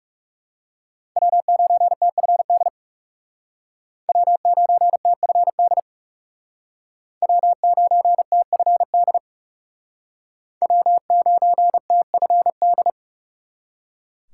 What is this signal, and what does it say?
Morse code, heard as a keyed tone, about 36 wpm, 700 Hz. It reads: W9TFD W9TFD W9TFD W9TFD
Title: Speed Racing - USA 1x3 Call Signs 30wpm